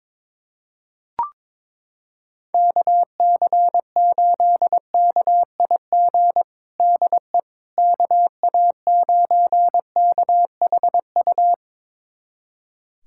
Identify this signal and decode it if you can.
Morse code, heard as a keyed tone, about 22 wpm, 700 Hz. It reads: KC8KIG DE KA9KHU
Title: USA Callsign DE Callsign - 6 Characters - 22wpm